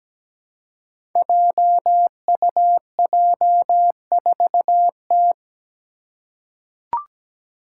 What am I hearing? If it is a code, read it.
Morse code, heard as a keyed tone, about 17 wpm, 700 Hz. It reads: JUJ4T
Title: Five Challenging Letter-Number Combinations 17wpm